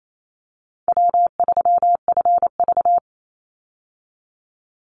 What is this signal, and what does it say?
Morse code, heard as a keyed tone, about 28 wpm, 700 Hz. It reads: W3FV
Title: Easy Call Signs Repeated 3x - 28wpm